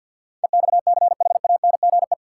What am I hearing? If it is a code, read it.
Morse code, heard as a keyed tone, about 53 wpm, 700 Hz. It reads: EXCHANGE
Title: Sets of 5 Words - New Word Drill - 50wpm